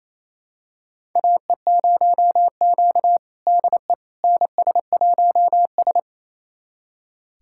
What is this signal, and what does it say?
Morse code, heard as a keyed tone, about 28 wpm, 700 Hz. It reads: AE0Q DE NS1S
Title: USA Callsign DE Callsign - 4 Characters - 28wpm